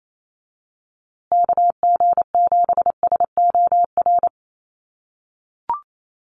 Morse code, heard as a keyed tone, about 28 wpm, 700 Hz. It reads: KG7SOR
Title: USA Call Signs Repeated 2x 28wpm